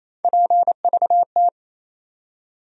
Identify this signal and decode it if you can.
Morse code, heard as a keyed tone, about 28 wpm, 700 Hz. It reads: PVT
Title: USA Call Sign Postfixes - 3 Letters 28wpm